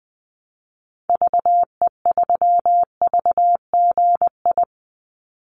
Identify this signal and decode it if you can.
Morse code, heard as a keyed tone, about 20 wpm, 700 Hz. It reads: VE3VGI
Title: Call Signs Repeated 3x - 20wpm